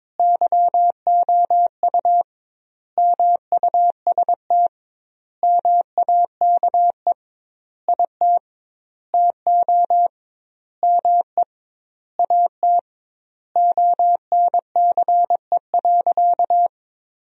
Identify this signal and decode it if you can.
Morse code, heard as a keyed tone, about 22 wpm, 700 Hz. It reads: YOU MUST MAKE IT TO ME AT ONCE.
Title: 2x Word-Spacing - Sentences from Top 500 Words - 22wpm